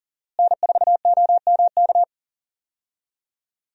Morse code, heard as a keyed tone, about 40 wpm, 700 Hz. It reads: N4OMK
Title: USA Call Signs - 5 Characters - Repeated 3x - 40wpm